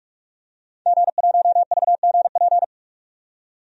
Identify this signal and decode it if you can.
Morse code, heard as a keyed tone, about 46 wpm, 700 Hz. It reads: G1VGP
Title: DXCC-100 Call Signs - 5 Characters - 45wpm